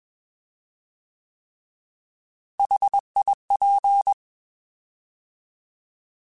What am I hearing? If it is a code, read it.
Morse code, heard as a keyed tone, about 21 wpm, 800 Hz. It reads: HIP